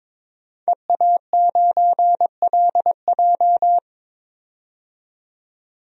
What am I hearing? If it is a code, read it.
Morse code, heard as a keyed tone, about 22 wpm, 700 Hz. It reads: EA9LJ
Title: DXCC-100 Call Signs - 5 Characters - 22wpm